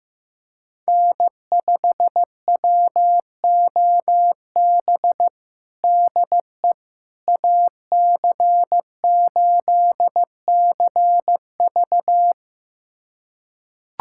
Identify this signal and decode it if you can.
Morse code, heard as a keyed tone, about 15 wpm, 700 Hz. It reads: N5WOB DE AC8CV
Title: USA Callsign DE Callsign - 5 Characters - 15wpm